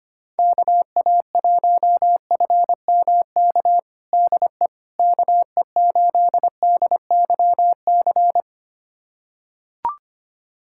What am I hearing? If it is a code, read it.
Morse code, heard as a keyed tone, about 25 wpm, 700 Hz. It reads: KA1FMK DE KE8DYC
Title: USA Callsign DE Callsign - 6 Characters - 25wpm